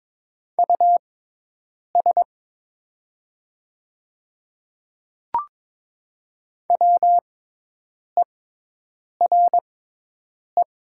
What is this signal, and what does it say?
Morse code, heard as a keyed tone, about 22 wpm, spaced as for 8 wpm, 700 Hz. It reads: US WERE
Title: 6x Character Spacing - Top 300 Words - 22wpm